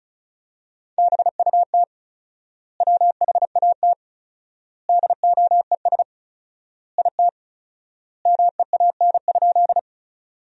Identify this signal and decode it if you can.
Morse code, heard as a keyed tone, about 36 wpm, 700 Hz. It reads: BUT WHAT DOES IT MEAN?
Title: 4x Word-Spacing - Sentences from Top 200 Words - 35wpm